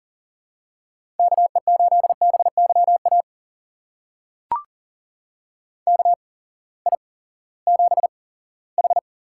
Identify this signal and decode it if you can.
Morse code, heard as a keyed tone, about 40 wpm, spaced as for 20 wpm, 700 Hz. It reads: KE8BYA KI7H
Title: USA Call Signs - 6 Characters - 8x Character Spacing - 40wpm